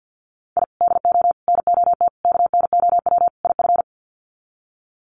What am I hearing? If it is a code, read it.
Morse code, heard as a keyed tone, about 50 wpm, 700 Hz. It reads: I DO NOT KNOW IF
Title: Sets of 5 Words 50wpm